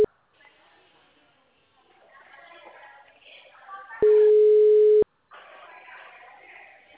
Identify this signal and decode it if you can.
Ringback tone (Europe, 425 Hz), with speech.
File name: out- -2002- - - .74042